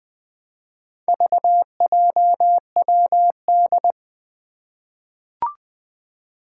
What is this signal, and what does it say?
Morse code, heard as a keyed tone, about 20 wpm, 700 Hz. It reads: VJWD
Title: Four Challenging Letter-Number Combinations 20wpm